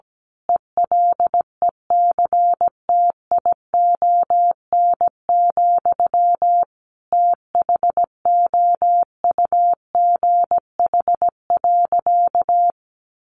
Morse code, heard as a keyed tone, about 17 wpm, 700 Hz. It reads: ELECTION, THOUGH.
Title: Sentences from Top 1500 Words 17wpm